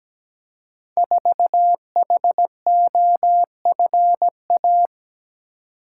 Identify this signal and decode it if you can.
Morse code, heard as a keyed tone, about 17 wpm, 700 Hz. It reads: 4HOFA
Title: Code Groups - 5 Letter:Numbers - 17wpm